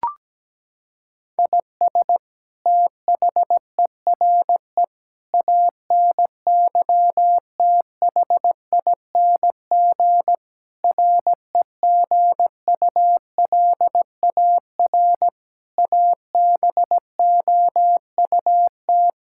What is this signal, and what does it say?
Morse code, heard as a keyed tone, about 17 wpm, 700 Hz. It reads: IS THERE ANYTHING REGULAR ABOUT
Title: Sentences from Top 1000 Words - Review Part 6 - 17wpm